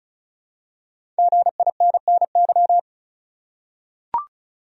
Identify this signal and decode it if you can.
Morse code, heard as a keyed tone, about 34 wpm, 700 Hz. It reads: GINNY
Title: USA Common Names - 35wpm